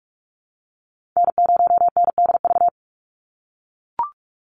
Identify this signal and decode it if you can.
Morse code, heard as a keyed tone, about 44 wpm, 700 Hz. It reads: N0NDV
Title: USA Call Signs 45wpm